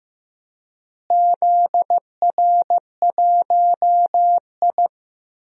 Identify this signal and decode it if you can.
Morse code, heard as a keyed tone, about 15 wpm, 700 Hz. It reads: ZR1I
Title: DXCC-100 Call Signs - 4 Characters - Repeated 3x - 15wpm